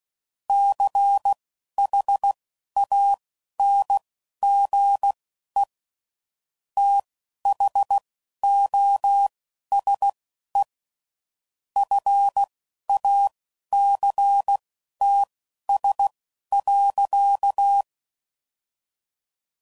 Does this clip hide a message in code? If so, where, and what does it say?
Yes: Morse code from 0.5 to 17.8 s, heard as a keyed tone, about 16 wpm, spaced as for 11 wpm, 800 Hz. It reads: CHANGE THOSE FACTS.